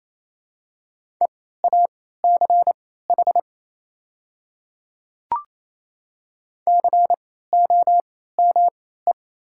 Morse code, heard as a keyed tone, about 28 wpm, spaced as for 16 wpm, 700 Hz. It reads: EACH COME
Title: 3x Character Spacing - Top 100 Words - 28wpm